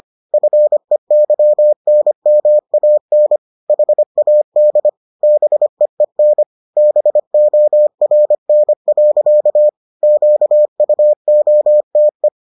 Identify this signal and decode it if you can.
Morse code, heard as a keyed tone, about 25 wpm, 600 Hz. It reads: FEYNMAN HAD BEEN BORN. QUOTE